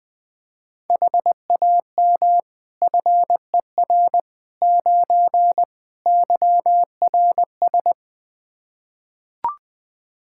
Morse code, heard as a keyed tone, about 20 wpm, 700 Hz. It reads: HAM FER 9 YRS
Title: K1USN SST QSO Practice 20wpm